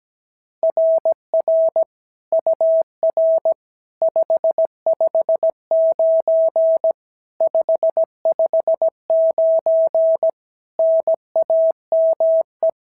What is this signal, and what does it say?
Morse code, heard as a keyed tone, about 16 wpm, 650 Hz. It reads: RR UR 559 559 NAME
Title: SOTA QSO Practice 17wpm